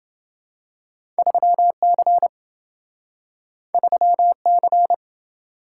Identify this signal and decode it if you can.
Morse code, heard as a keyed tone, about 28 wpm, 700 Hz. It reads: 3C 3C